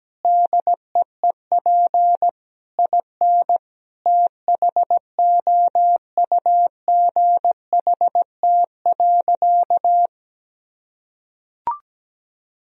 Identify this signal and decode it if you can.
Morse code, heard as a keyed tone, about 17 wpm, 700 Hz. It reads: DEEP IN THOUGHT.
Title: Sentences from Top 1000 Words - Review Part 3 - 17wpm